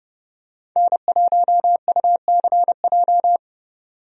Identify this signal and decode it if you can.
Morse code, heard as a keyed tone, about 30 wpm, 700 Hz. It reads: N1UCJ
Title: USA Call Signs - 5 Characters - 30wpm